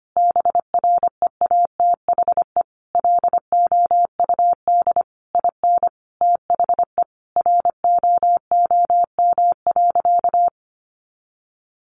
Morse code, heard as a keyed tone, about 25 wpm, 700 Hz. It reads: BREATHE LOUD IN THE ROOM.
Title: A Sound Of Thunder 25wpm